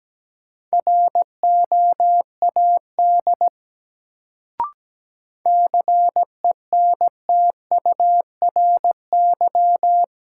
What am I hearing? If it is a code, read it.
Morse code, heard as a keyed tone, about 17 wpm, 700 Hz. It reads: ROAD CENTURY
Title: Sets of 5 Words - New Word Drill - 17wpm